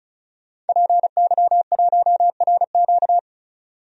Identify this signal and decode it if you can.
Morse code, heard as a keyed tone, about 36 wpm, 700 Hz. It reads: PY1RQ
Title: DXCC-100 Call Signs - 5 Characters - 35wpm